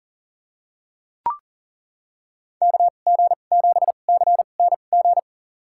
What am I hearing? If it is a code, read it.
Morse code, heard as a keyed tone, about 40 wpm, spaced as for 29 wpm, 700 Hz. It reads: KG7CNG
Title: USA Call Signs - 2x Character Spacing - 40wpm